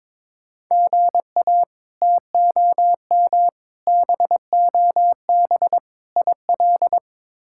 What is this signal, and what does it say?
Morse code, heard as a keyed tone, about 22 wpm, 700 Hz. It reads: GA TOM BOB IL
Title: K1USN SST QSO Practice - 22wpm